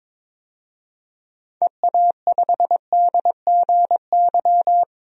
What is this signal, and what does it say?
Morse code, heard as a keyed tone, about 22 wpm, 700 Hz. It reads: EA5DGY
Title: DXCC-100 Call Signs - 6 Characters - 22wpm